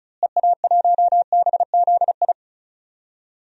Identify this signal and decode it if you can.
Morse code, heard as a keyed tone, about 34 wpm, 700 Hz. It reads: EA1BZI
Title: DXCC-100 Call Signs - 6 Characters - Repeated 3x - 35wpm